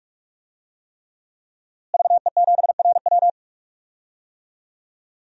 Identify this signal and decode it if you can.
Morse code, heard as a keyed tone, about 41 wpm, 700 Hz. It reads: VE7RW
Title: DXCC-100 Call Signs - 5 Characters - Repeated 3x - 45wpm